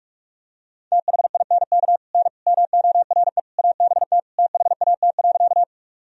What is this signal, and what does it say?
Morse code, heard as a keyed tone, about 46 wpm, 700 Hz. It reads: THINK N MORE ABT THAT.
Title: Sentences from Top 200 Words and Abbreviations 45wpm